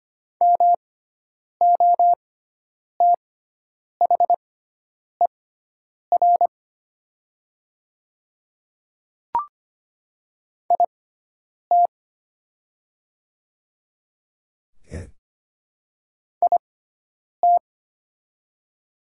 Morse code, heard as a keyed tone, about 25 wpm, spaced as for 9 wpm, 700 Hz. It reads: MOTHER IT IT
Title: 6x Character Spacing - Top 200 Words - 25wpm